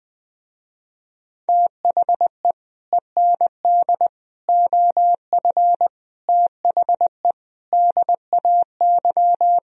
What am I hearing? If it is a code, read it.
Morse code, heard as a keyed tone, about 20 wpm, 700 Hz. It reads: THE END OF THE DAY
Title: Speed-Racing - Sets of 5 Words 20wpm